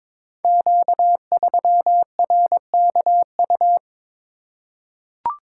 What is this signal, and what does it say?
Morse code, heard as a keyed tone, about 22 wpm, 700 Hz. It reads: Q3RKU
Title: Code Groups - 5 Letter:Numbers - 22wpm